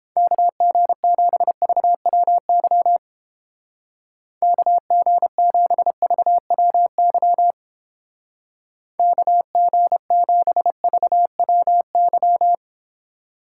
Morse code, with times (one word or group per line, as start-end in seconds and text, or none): 0.2-3.0 KG7VWY
4.4-7.5 KG7VWY
9.0-12.5 KG7VWY